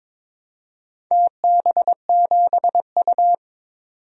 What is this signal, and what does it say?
Morse code, heard as a keyed tone, about 22 wpm, 700 Hz. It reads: TB7U
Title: DXCC-100 Call Signs - 4 Characters - 22wpm